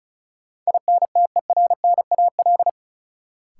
Morse code, heard as a keyed tone, about 36 wpm, 700 Hz. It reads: INTERNAL
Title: Sentences from Top 1400 Words - New Word Drill - 35wpm